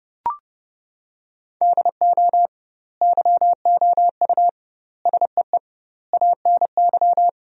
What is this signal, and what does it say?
Morse code, heard as a keyed tone, about 30 wpm, 700 Hz. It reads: DO YOU SEE ANY
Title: 2x Word-Spacing - Sentences from Top 500 Words - 30wpm